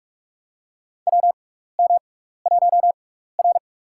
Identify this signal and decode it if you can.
Morse code, heard as a keyed tone, about 44 wpm, spaced as for 15 wpm, 700 Hz. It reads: WM1R
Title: USA Call Signs - 4 Characters - 6x Character Spacing - 45wpm